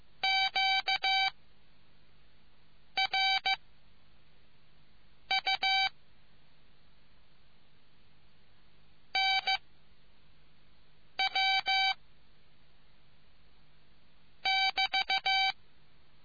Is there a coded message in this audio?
Yes: Morse code from 0.2 to 15.5 s, heard as a keyed tone, about 15 wpm, 770 Hz. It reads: Q R U N W <BT>